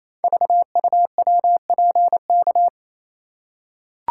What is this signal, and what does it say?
Morse code, heard as a keyed tone, about 27 wpm, 700 Hz. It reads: VUWPK
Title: Code Groups - 5 Letters - 28wpm